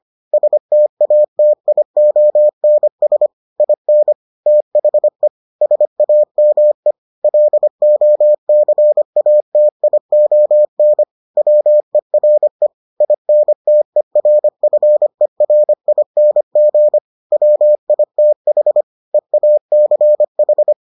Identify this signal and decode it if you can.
Morse code, heard as a keyed tone, about 25 wpm, 600 Hz. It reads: STATIONS IN THE SAME LOCATION WERE INTERFERING WITH EACH